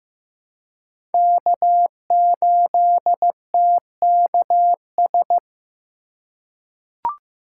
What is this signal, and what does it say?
Morse code, heard as a keyed tone, about 15 wpm, 700 Hz. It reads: K8TKS
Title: USA Call Signs - 5 Characters - 2x Character Spacing - 15wpm